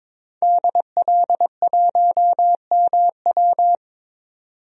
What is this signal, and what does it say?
Morse code, heard as a keyed tone, about 21 wpm, 700 Hz. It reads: DL1MW
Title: DXCC-100 Call Signs - 5 Characters - 22wpm